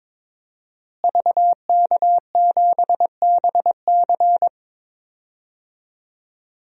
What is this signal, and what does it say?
Morse code, heard as a keyed tone, about 22 wpm, 700 Hz. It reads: VK7BC
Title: DXCC-100 Call Signs - 5 Characters - Repeated 3x - 22wpm